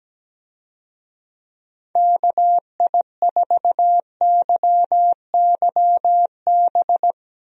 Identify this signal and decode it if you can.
Morse code, heard as a keyed tone, about 17 wpm, 700 Hz. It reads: KI4YYB
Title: USA Call Signs - 6 Characters - Repeated 3x - 17wpm